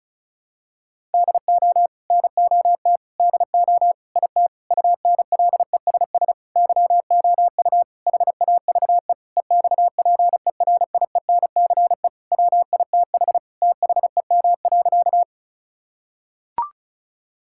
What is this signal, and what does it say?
Morse code, heard as a keyed tone, about 34 wpm, 700 Hz. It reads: DO NOT DO IT UNLESS YOU HAVE EXPERIENCE WITH THEM.
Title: Sentences from Top 1000 Words - Review Part 6 - 35wpm